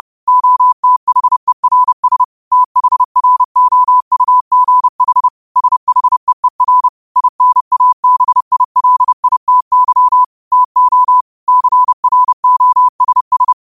Morse code, heard as a keyed tone, about 30 wpm, 1000 Hz. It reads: OTHERS THROUGH SHEER INABILITY TO CROSS
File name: qotdfiles / 30wpm